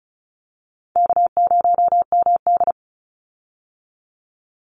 Morse code, heard as a keyed tone, about 34 wpm, 700 Hz. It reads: K0MD
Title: Call Signs - 35wpm